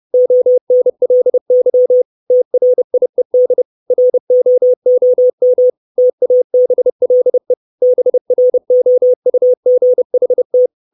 Morse code, heard as a keyed tone, about 30 wpm, 500 Hz. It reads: ONLY TRIED ROOM TABLE BROUGHT
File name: 632 Words 30WPM